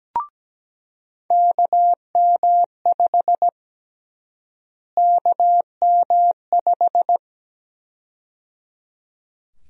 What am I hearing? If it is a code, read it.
Morse code, heard as a keyed tone, about 17 wpm, 700 Hz. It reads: KM5 KM5
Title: USA Call Sign Prefixes - Repeated 2x - 17wpm